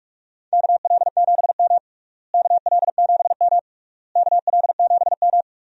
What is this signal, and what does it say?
Morse code, heard as a keyed tone, about 41 wpm, 700 Hz. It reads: KL7M KL7M KL7M
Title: DXCC-100 Call Signs - 4 Characters - Repeated 3x - 45wpm